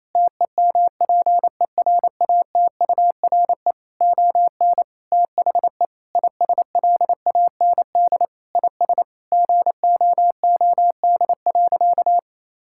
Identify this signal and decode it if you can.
Morse code, heard as a keyed tone, about 27 wpm, 700 Hz. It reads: TEMPERATURE ON THE ISLAND IS GOOD.
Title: Sentences from Top 1000 Words - Review Part 6 - 28wpm